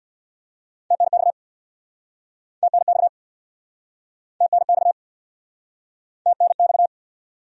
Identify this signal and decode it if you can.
Morse code, heard as a keyed tone, about 69 wpm, 700 Hz. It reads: TNX TNX TNX TNX